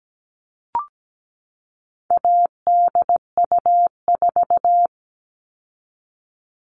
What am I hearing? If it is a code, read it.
Morse code, heard as a keyed tone, about 17 wpm, 700 Hz. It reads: ADU4
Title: All Cut Numbers - 4-Digit 17wpm